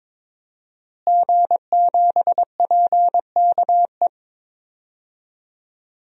Morse code, heard as a keyed tone, about 22 wpm, 700 Hz. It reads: G7PKE
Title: DXCC-100 Call Signs - 5 Characters - Repeated 3x - 22wpm